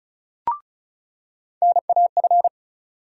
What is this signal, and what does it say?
Morse code, heard as a keyed tone, about 36 wpm, 700 Hz. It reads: NAF
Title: USA Call Sign Postfixes - 3 Letters 35wpm